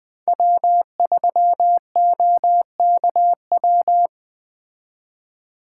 Morse code, heard as a keyed tone, about 20 wpm, 700 Hz. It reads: W3OKW
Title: Speed Racing - USA Call Signs 17wpm